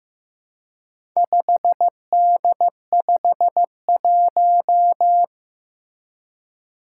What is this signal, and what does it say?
Morse code, heard as a keyed tone, about 15 wpm, 700 Hz. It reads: 5D51
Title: Four Challenging Letter-Number Combinations 15wpm